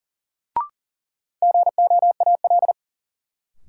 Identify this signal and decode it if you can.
Morse code, heard as a keyed tone, about 40 wpm, 700 Hz. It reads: GOAL
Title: Sentences from Top 1000 Words - Review Part 5 - New Word Drill - 40wpm